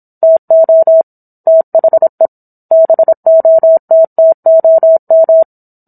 Morse code, heard as a keyed tone, about 27 wpm, 650 Hz. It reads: TO THE BOTTOM
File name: Book One / 17-Chapter 17 - 26 WPM